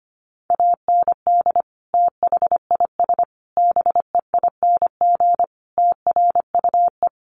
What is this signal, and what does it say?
Morse code, heard as a keyed tone, about 25 wpm, 700 Hz. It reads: AND THIS BEING TRUE
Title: Sentences from Top 1400 Words 25wpm